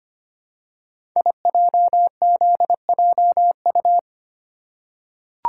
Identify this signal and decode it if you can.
Morse code, heard as a keyed tone, about 25 wpm, 700 Hz. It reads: IJZJU
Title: Code Groups - 5 Letter:Numbers - 25wpm